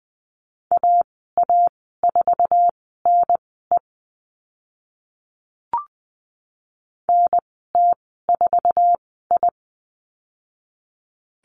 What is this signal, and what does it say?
Morse code, heard as a keyed tone, about 20 wpm, spaced as for 15 wpm, 700 Hz. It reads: AA4NE NT4I
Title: 062 Lesson 11g - Letter h - Call Signs 20wpm